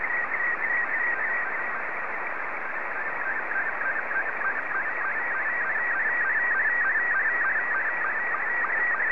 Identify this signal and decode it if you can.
SSTV (slow-scan television), Robot 72.
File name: sstv / R72-ZL2CX